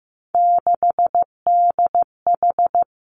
Morse code, heard as a keyed tone, about 15 wpm, 700 Hz. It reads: 6DH
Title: Three Challenging Letter-Number Combinations 15wpm